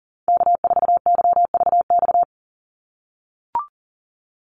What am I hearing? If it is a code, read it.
Morse code, heard as a keyed tone, about 40 wpm, 700 Hz. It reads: K4YVX
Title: USA Call Signs 40wpm